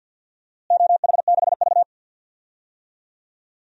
Morse code, heard as a keyed tone, about 52 wpm, 700 Hz. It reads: OH6V
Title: DXCC-100 Call Signs - 4 Characters - Repeated 2x - 50wpm